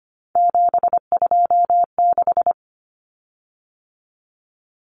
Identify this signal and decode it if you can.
Morse code, heard as a keyed tone, about 25 wpm, 700 Hz. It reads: 726
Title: Common Cut Numbers - 3-Digit 25wpm